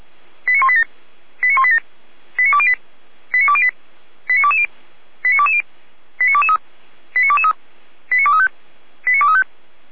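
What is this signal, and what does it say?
Five-tone selective call (ZVEI), heard as quick runs of five tones, tones of about 70 ms. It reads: 89187 89187 89208 89208 89220 89220 89223 89223 89235 89235
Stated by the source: ZVEI five-tone selective call received on an SDR, recorded by Feuerwehr Orlamünde (f85360).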